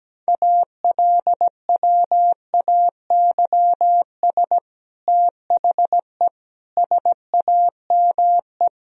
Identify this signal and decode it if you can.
Morse code, heard as a keyed tone, about 17 wpm, 700 Hz. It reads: ALWAYS THE SAME
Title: Sentences from Top 300 Words and Abbreviations 17wpm